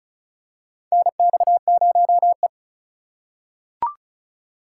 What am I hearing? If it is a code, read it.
Morse code, heard as a keyed tone, about 36 wpm, 700 Hz. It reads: NX0E
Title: USA Call Signs - 4 Characters - 2x Character Spacing - 35wpm